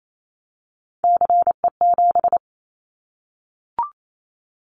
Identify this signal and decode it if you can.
Morse code, heard as a keyed tone, about 28 wpm, 700 Hz. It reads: CE7
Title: Callsign Prefixes for Top 100 DXCC - 28wpm